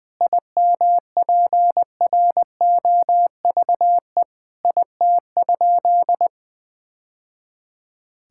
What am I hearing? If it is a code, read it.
Morse code, heard as a keyed tone, about 20 wpm, 700 Hz. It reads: IMPROVE IT?
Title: Sentences from Top 1000 Words - Review Part 5 - 20wpm